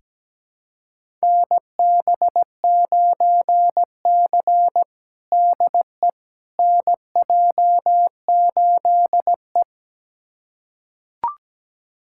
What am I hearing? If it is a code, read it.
Morse code, heard as a keyed tone, about 17 wpm, 700 Hz. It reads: NB9C DE NJ8E
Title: USA Callsign DE Callsign - 4 Characters - 17wpm